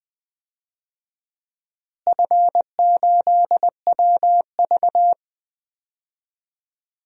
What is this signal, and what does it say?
Morse code, heard as a keyed tone, about 20 wpm, 700 Hz. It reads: F8WV
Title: DXCC-100 Call Signs - 4 Characters - Repeated 3x - 20wpm